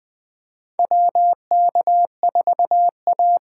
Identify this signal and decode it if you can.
Morse code, heard as a keyed tone, about 20 wpm, 700 Hz. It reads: WK4A
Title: USA Call Signs - 4x Character Spacing - 20wpm